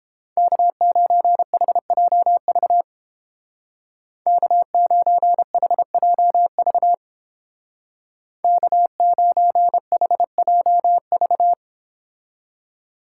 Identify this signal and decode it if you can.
Morse code, heard as a keyed tone, about 30 wpm, 700 Hz. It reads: K9HJV K9HJV K9HJV